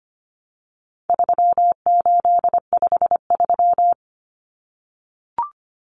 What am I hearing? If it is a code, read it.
Morse code, heard as a keyed tone, about 25 wpm, 700 Hz. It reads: 3853
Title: Common Cut Numbers - 4-Digit 25wpm